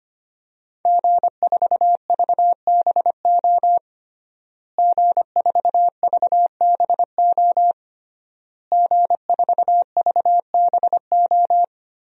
Morse code, heard as a keyed tone, about 25 wpm, 700 Hz. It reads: G4VBO G4VBO G4VBO
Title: DXCC-100 Call Signs - 5 Characters - Repeated 3x - 25wpm